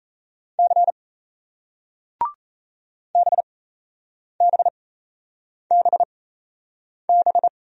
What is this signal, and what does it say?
Morse code, heard as a keyed tone, about 38 wpm, spaced as for 8 wpm, 700 Hz. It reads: C BBBB